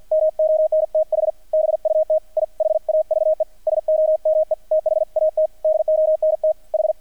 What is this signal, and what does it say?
Morse code, heard as a keyed tone, about 48 wpm, 630 Hz, with radio noise. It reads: MONTH BUT I HAVE SOME THAT DONT H
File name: archived / CRSnet